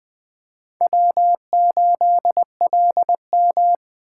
Morse code, heard as a keyed tone, about 20 wpm, 700 Hz. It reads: W8LM
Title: Speed Racing - USA 1x2 Call Signs 17wpm